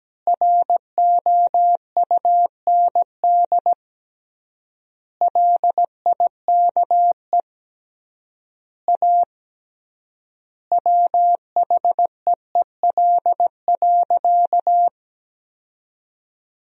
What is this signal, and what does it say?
Morse code, heard as a keyed tone, about 17 wpm, 700 Hz. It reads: ROUND LIKE A WHEEL.
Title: 3x Word-Spacing - Sentences from Top 500 Words - 17wpm